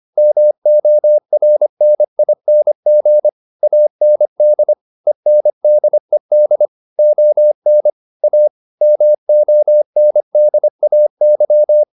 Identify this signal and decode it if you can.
Morse code, heard as a keyed tone, about 25 wpm, 600 Hz. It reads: MORNING AND ENDED ON A MONDAY